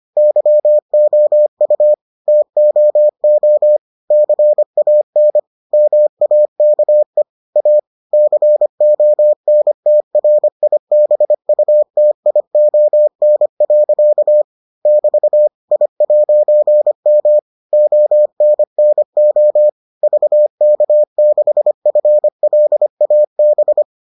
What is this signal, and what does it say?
Morse code, heard as a keyed tone, about 25 wpm, 600 Hz. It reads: YOU TOO CAN MAKE A CONTRIBUTION. <BT> I'M ONNO VK6FLAB